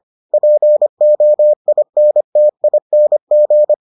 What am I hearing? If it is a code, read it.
Morse code, heard as a keyed tone, about 25 wpm, 600 Hz. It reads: POINTING